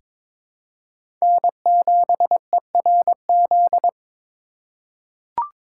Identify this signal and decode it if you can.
Morse code, heard as a keyed tone, about 21 wpm, 700 Hz. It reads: N7ERZ
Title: USA Call Signs - 5 Characters - 4x Character Spacing - 22wpm